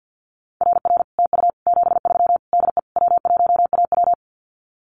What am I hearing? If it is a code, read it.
Morse code, heard as a keyed tone, about 50 wpm, 700 Hz. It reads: RR TU 73 DE W1AW
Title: Common QSO Phrases Encore 50wpm